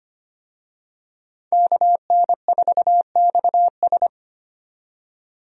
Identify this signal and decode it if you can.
Morse code, heard as a keyed tone, about 25 wpm, 700 Hz. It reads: KN4XS
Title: USA Call Signs - 5 Characters - 25wpm